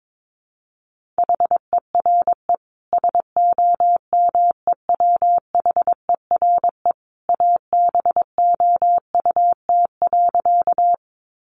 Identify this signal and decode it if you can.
Morse code, heard as a keyed tone, about 22 wpm, 700 Hz. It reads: HERE SOMEWHERE ABOUT.
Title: Sentences from Top 1400 Words 22wpm